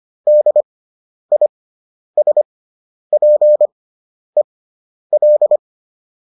Morse code, heard as a keyed tone, about 25 wpm, spaced as for 10 wpm, 600 Hz. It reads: DISPEL